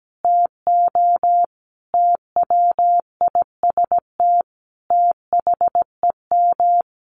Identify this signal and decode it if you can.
Morse code, heard as a keyed tone, about 17 wpm, 700 Hz. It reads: TO TWIST THEM
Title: A Sound Of Thunder 17wpm